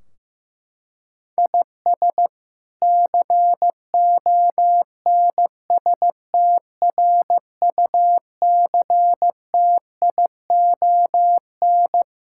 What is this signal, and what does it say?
Morse code, heard as a keyed tone, about 16 wpm, 700 Hz. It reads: IS CONSTRUCTION
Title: Sentences from Top 1600 Words 15wpm